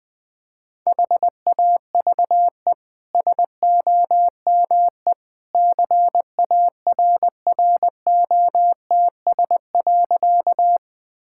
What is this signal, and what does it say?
Morse code, heard as a keyed tone, about 20 wpm, 700 Hz. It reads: HAVE SOME CARROTS.
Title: Easy Sentences with Hints - 20wpm